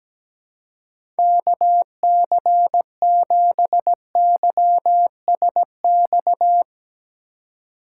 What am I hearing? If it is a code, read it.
Morse code, heard as a keyed tone, about 17 wpm, 700 Hz. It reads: KC7YSX
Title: USA Call Signs - 6 Characters - 2x Character Spacing - 17wpm